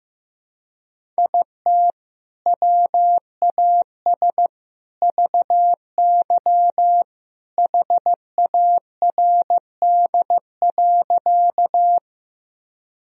Morse code, heard as a keyed tone, about 15 wpm, 700 Hz. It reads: IT WAS VY HARD.
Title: Sentences from Top 300 Words and Abbreviations 15wpm